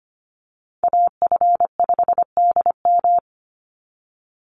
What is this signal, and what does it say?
Morse code, heard as a keyed tone, about 25 wpm, 700 Hz. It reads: AF5DM
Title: Call Signs - 25wpm